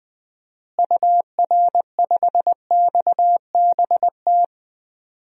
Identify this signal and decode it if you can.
Morse code, heard as a keyed tone, about 21 wpm, 700 Hz. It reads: UR5XBT
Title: DXCC-100 Call Signs - 6 Characters - Repeated 3x - 20wpm